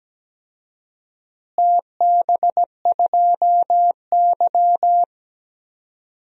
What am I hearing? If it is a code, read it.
Morse code, heard as a keyed tone, about 17 wpm, 700 Hz. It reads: TB2Y
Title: DXCC-100 Call Signs - 4 Characters - Repeated 2x - 17wpm